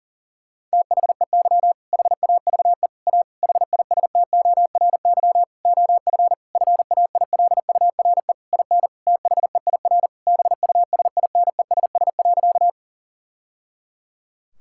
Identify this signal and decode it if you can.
Morse code, heard as a keyed tone, about 40 wpm, 700 Hz. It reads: THEY HAVE A HISTORY OF FAILURE IN THEIR BUSINESS.
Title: Sentences from Top 1000 Words - Review Part 6 - 40wpm